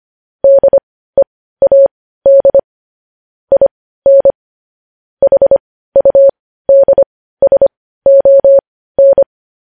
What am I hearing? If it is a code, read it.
Morse code, heard as a keyed tone, about 25 wpm, spaced as for 15 wpm, 550 Hz. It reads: DEAD IN HUDSON
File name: podcast / News headlines-1x-25x15